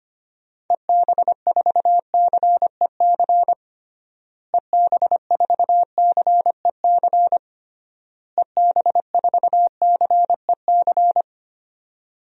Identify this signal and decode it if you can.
Morse code, heard as a keyed tone, about 25 wpm, 700 Hz. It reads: EB4CEC EB4CEC EB4CEC
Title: DXCC-100 Call Signs - 6 Characters - Repeated 3x - 25wpm